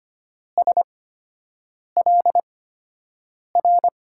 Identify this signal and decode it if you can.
Morse code, heard as a keyed tone, about 25 wpm, spaced as for 7 wpm, 700 Hz. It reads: SLR
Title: USA Call Signs - 6 Characters - 8x Character Spacing - 25wpm